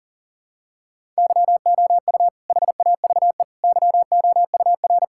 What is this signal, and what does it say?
Morse code, heard as a keyed tone, about 40 wpm, 700 Hz. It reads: YOU HAVE YOUR
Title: Sentences from Top 1000 Words - Review Part 5 - 40wpm